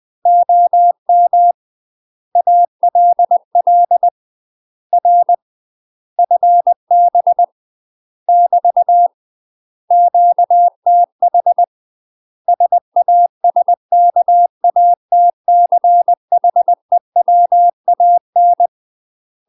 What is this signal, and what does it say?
Morse code, heard as a keyed tone, about 20 wpm, 700 Hz. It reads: OM ALL R FB <BT> QTH SASKATCHEWAN
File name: UK MASTER QSOs TRAINING / 20WPM / UK MASTER QSOs 0013